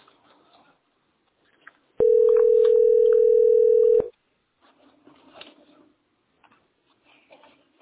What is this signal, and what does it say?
Ringback tone (North America), 440 + 480 Hz.